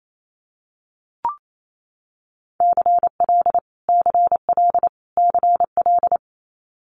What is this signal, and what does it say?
Morse code, heard as a keyed tone, about 28 wpm, 700 Hz. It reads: CL CL CL
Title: Multi-letter Phonemes Repeated 3x 28wpm